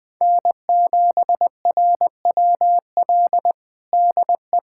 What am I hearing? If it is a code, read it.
Morse code, heard as a keyed tone, about 20 wpm, 700 Hz. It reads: N7RWL DE
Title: USA Callsign DE Callsign 20wpm